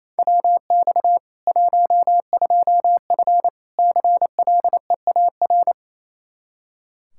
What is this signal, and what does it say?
Morse code, heard as a keyed tone, about 28 wpm, 700 Hz. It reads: WX 12F CLEAR
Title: Practice QSO - Exchange 2 - Info - 28wpm